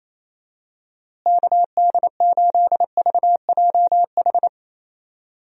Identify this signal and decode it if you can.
Morse code, heard as a keyed tone, about 27 wpm, 700 Hz. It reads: KD8VJH
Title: USA Call Signs - 6 Characters - Repeated 2x - 28wpm